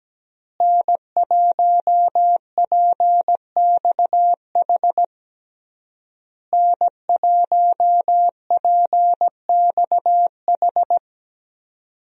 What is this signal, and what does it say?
Morse code, heard as a keyed tone, about 17 wpm, 700 Hz. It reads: N1PXH N1PXH
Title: USA Call Signs - 5 Characters - Repeated 3x - 17wpm